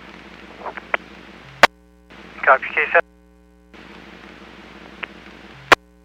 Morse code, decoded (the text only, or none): none